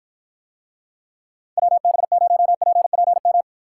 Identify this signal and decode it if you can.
Morse code, heard as a keyed tone, about 53 wpm, 700 Hz. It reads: WB0PPM